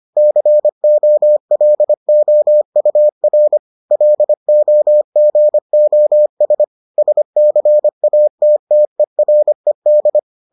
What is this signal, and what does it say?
Morse code, heard as a keyed tone, about 25 wpm, 600 Hz. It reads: COLOUR LOGOS SCATTERED